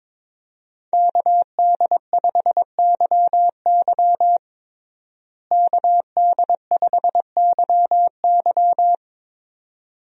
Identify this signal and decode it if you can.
Morse code, heard as a keyed tone, about 22 wpm, 700 Hz. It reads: KD5YY KD5YY
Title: USA Call Signs - 5 Characters - Repeated 2x - 22wpm